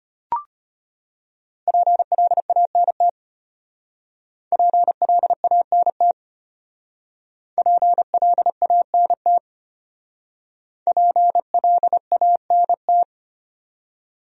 Morse code, heard as a keyed tone, about 34 wpm, 700 Hz. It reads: PLANT PLANT PLANT PLANT